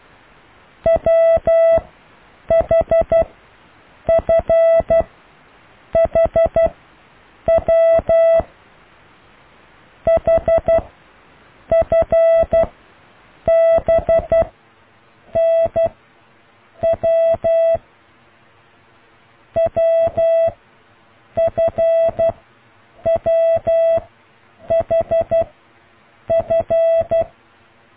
Morse code, heard as a keyed tone, about 12 wpm, spaced as for 7 wpm, 660 Hz. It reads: WHFHW HFBNW WFWHF